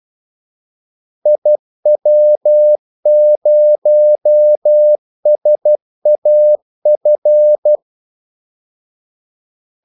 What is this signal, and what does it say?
Morse code, heard as a keyed tone, about 12 wpm, 600 Hz. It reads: IW0SAF
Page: 2 ekstra ordmellemrum mellem kaldesignalerne